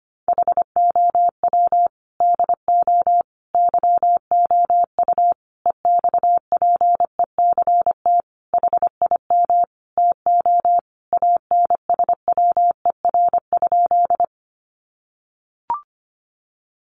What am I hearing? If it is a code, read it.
Morse code, heard as a keyed tone, about 25 wpm, 700 Hz. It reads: HOW DO YOU EXPECT HIM TO ANSWER?
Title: Sentences from Top 1100 Words 25wpm